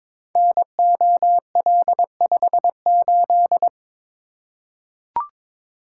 Morse code, heard as a keyed tone, about 22 wpm, 700 Hz. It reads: NOL58
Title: Code Groups - 5 Letter:Numbers - 22wpm